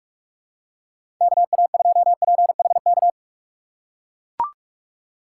Morse code, heard as a keyed tone, about 44 wpm, 700 Hz. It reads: KA2PHK
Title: USA Call Signs - 4x Character Spacing - 45wpm